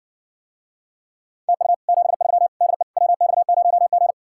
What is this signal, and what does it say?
Morse code, heard as a keyed tone, about 60 wpm, 700 Hz. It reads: TU 73 DE WX0G